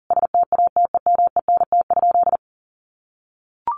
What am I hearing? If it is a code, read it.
Morse code, heard as a keyed tone, about 40 wpm, 700 Hz. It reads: STATEMENT?
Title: Sentences from Top 1200 Words 40wpm